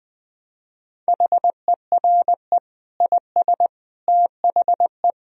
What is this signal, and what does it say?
Morse code, heard as a keyed tone, about 20 wpm, 700 Hz. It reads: HERE IS THE
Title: Sentences from Top 1000 Words - Review Part 4 - 20wpm